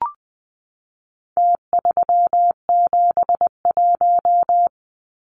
Morse code, heard as a keyed tone, about 20 wpm, 700 Hz. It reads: T371
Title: Numbers - Contest Style - 4-Digit 20wpm